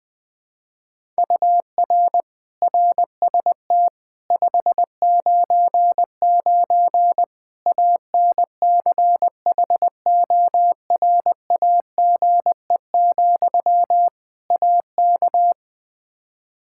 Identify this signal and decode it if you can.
Morse code, heard as a keyed tone, about 21 wpm, 700 Hz. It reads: UR RST 599 ANCHORAGE, AK
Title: K1USN SST QSO Practice 20wpm